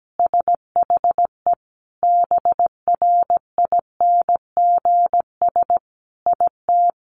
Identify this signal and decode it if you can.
Morse code, heard as a keyed tone, about 17 wpm, 700 Hz. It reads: SHE BRINGS IT
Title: Easy Phrases 17wpm